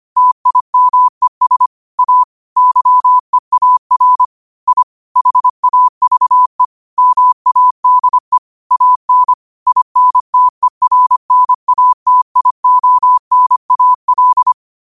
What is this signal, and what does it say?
Morse code, heard as a keyed tone, about 25 wpm, 1000 Hz. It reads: TIMES A YEAR I HAVE MADE AN INTERNATIONAL
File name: qotdfiles / 25wpm